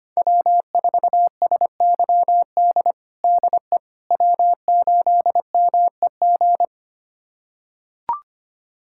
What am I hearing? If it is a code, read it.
Morse code, heard as a keyed tone, about 25 wpm, 700 Hz. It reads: W4SYD DE W8MEG
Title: USA Callsign DE Callsign - 5 Characters - 25wpm